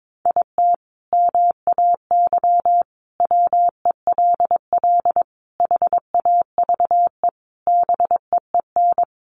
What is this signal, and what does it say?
Morse code, heard as a keyed tone, about 22 wpm, 700 Hz. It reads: IT MAY WELL HAVE BEEN
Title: Sentences from Top 1000 Words 22wpm